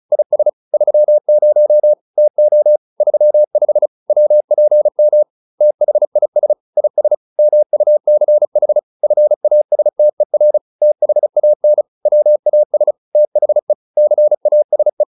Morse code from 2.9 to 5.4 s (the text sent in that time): 35 WPM